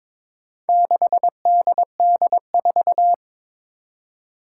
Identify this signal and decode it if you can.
Morse code, heard as a keyed tone, about 23 wpm, 700 Hz. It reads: 6DD4
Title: Four Challenging Letter-Number Combinations 22wpm